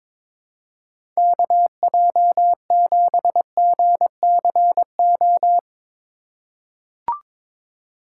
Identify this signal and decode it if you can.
Morse code, heard as a keyed tone, about 22 wpm, 700 Hz. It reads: KJ7GCO
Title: USA Call Signs - 6 Characters - 2x Character Spacing - 22wpm